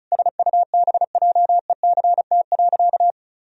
Morse code, heard as a keyed tone, about 34 wpm, 700 Hz. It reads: SUBJECT.
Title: Sentences from Top 1000 Words - Review Part 1 - 35wpm